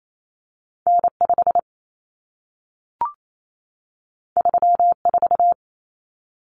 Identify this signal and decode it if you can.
Morse code, heard as a keyed tone, about 28 wpm, 700 Hz. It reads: N5 34
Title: Common Cut Numbers - 2-Digit 28wpm